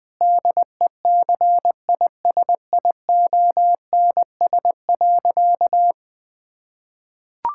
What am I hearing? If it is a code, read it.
Morse code, heard as a keyed tone, about 21 wpm, 700 Hz. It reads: DECISIONS.
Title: Sentences with 5-Word Phrases - 20wpm